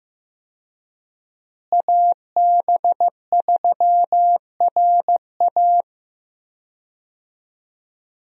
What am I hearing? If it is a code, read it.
Morse code, heard as a keyed tone, about 15 wpm, 700 Hz. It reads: AB3RA
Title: USA Call Signs - 5 Characters - Repeated 3x - 15wpm